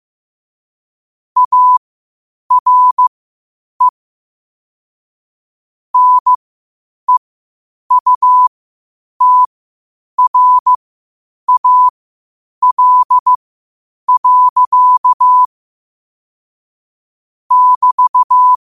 Morse code, heard as a keyed tone, about 15 wpm, spaced as for 8 wpm, 1000 Hz. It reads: ARE NEUTRAL. <BT>
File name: qotdfiles / 8wpm